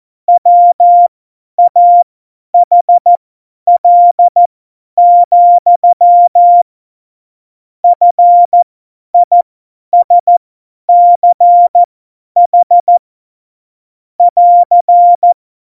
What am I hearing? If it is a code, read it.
Morse code, heard as a keyed tone, about 14 wpm, spaced as for 10 wpm, 700 Hz. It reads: WAHL, FISCH <AR>
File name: Morsekurs / Witze / Witz 13. November 2018, 50 F70